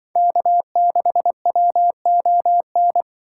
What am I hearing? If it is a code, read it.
Morse code, heard as a keyed tone, about 24 wpm, 700 Hz. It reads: K6WON
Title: Speed Racing - USA Call Signs 20wpm